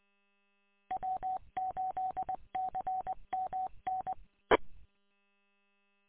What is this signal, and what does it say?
Morse code, heard as a keyed tone, about 20 wpm, 750 Hz. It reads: W8CMN E
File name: w8cmn- -combined